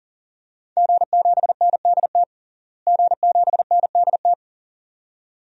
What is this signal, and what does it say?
Morse code, heard as a keyed tone, about 37 wpm, 700 Hz. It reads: G7NDT G7NDT
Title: DXCC-100 Call Signs - 5 Characters - Repeated 2x - 40wpm